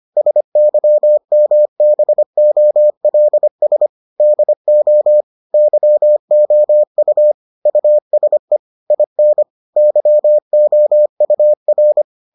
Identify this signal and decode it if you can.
Morse code, heard as a keyed tone, about 25 wpm, 600 Hz. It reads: SYMBOLS DO YOU USE IN YOUR